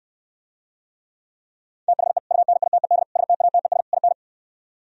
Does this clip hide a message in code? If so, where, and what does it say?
Yes: Morse code from 1.9 to 4.1 s, heard as a keyed tone, about 68 wpm, 700 Hz. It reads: THE UNITED STATES IN